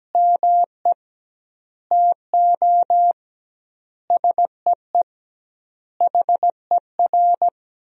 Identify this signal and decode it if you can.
Morse code, heard as a keyed tone, about 17 wpm, 700 Hz. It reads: ME TO SEE HER
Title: 2x Word-Spacing - Sentences from Top 500 Words - 17wpm